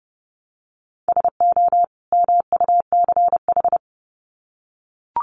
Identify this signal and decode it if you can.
Morse code, heard as a keyed tone, about 30 wpm, 700 Hz. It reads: SO MUCH
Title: Sets of 2 Words 30wpm